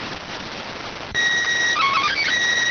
SSTV (slow-scan television), PD 290.